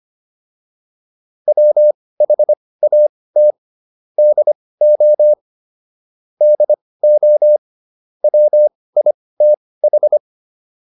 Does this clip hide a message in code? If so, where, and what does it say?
Yes: Morse code from 1.5 to 10.2 s, heard as a keyed tone, about 25 wpm, spaced as for 18 wpm, 600 Hz. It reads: WHAT DO DO WITH